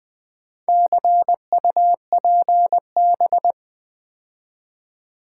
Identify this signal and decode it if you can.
Morse code, heard as a keyed tone, about 20 wpm, 700 Hz. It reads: CUPB
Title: Build A Word - 2 Letters At A Time - 8-Letter Words - 20wpm